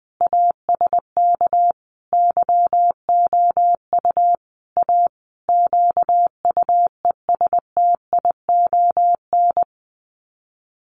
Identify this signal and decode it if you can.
Morse code, heard as a keyed tone, about 19 wpm, 700 Hz. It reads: ASK YOU A QUESTION
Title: Sets of 5 Words Encore2 20wpm